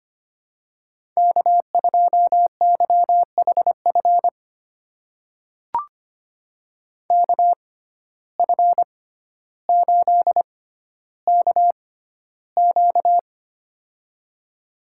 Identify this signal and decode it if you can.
Morse code, heard as a keyed tone, about 25 wpm, spaced as for 13 wpm, 700 Hz. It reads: K2YHF KF8KQ
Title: USA Call Signs - 5 Characters - 6x Character Spacing - 25wpm